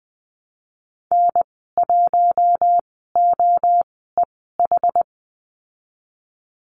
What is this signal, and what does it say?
Morse code, heard as a keyed tone, about 19 wpm, spaced as for 14 wpm, 700 Hz. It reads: N1OEH
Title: 062 Lesson 11g - Letter h - Call Signs 20wpm